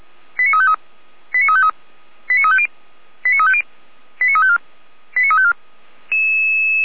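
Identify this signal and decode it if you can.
Five-tone selective call (ZVEI), heard as quick runs of five tones, tones of about 70 ms. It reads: 89353 89353 89350 89350 89354 89354
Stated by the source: Five-tone ZVEI selective call received by SDR.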